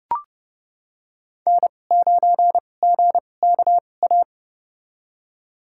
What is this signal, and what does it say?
Morse code, heard as a keyed tone, about 30 wpm, spaced as for 22 wpm, 700 Hz. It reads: N9GKA
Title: USA Call Signs - 5 Characters - 2x Character Spacing - 30wpm